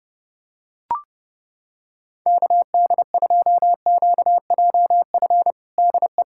Morse code, heard as a keyed tone, about 30 wpm, 700 Hz. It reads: KD2QJF DE
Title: USA Callsign DE Callsign - 6 Characters - 30wpm